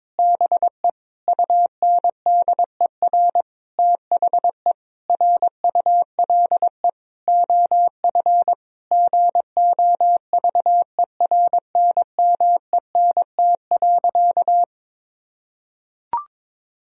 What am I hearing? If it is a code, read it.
Morse code, heard as a keyed tone, about 22 wpm, 700 Hz. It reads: BE UNDER THE RULE OF GOVERNMENT.
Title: Sentences from Top 1000 Words - Review Part 3 - 22wpm